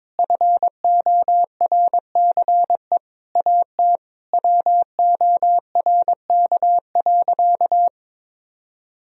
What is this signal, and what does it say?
Morse code, heard as a keyed tone, about 21 wpm, 700 Hz. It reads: FORCE AT WORK.
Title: Sentences from Top 1000 Words - Review Part 4 - 22wpm